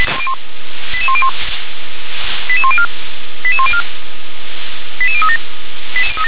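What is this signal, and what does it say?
Five-tone selective call (ZVEI), heard as quick runs of five tones, tones of about 70 ms. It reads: 88101 88104 88104 88037
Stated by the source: ZVEI five-tone call 88020 received by SDR.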